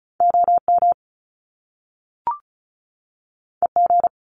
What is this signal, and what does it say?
Morse code, heard as a keyed tone, about 36 wpm, 700 Hz. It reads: OM EG
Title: Top Two Letter Combinations in Words 35wpm